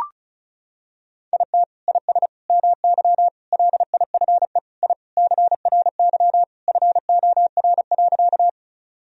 Morse code, heard as a keyed tone, about 34 wpm, 700 Hz. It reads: IT IS MY LIFE I CRY FOR.
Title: Sentences from Top 1000 Words - Review Part 2 - 35wpm